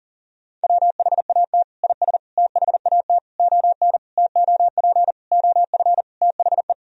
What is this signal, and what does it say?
Morse code, heard as a keyed tone, about 40 wpm, 700 Hz. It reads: WHAT IS THAT ON TOP OF THE
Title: Sentences from Top 1000 Words - Review Part 2 - 40wpm